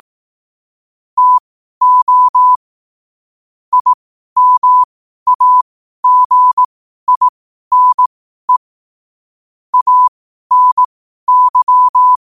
Morse code, heard as a keyed tone, about 18 wpm, spaced as for 12 wpm, 1000 Hz. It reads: TO IMAGINE ANY
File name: qotdfiles / 12wpm